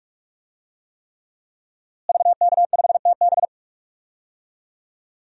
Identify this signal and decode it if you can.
Morse code, heard as a keyed tone, about 46 wpm, 700 Hz. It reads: VK5TB
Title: DXCC-100 Call Signs - 5 Characters - Repeated 3x - 45wpm